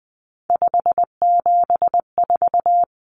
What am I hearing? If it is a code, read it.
Morse code, heard as a keyed tone, about 20 wpm, 700 Hz. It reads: 574
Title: Common Cut Numbers - 3-Digit 20wpm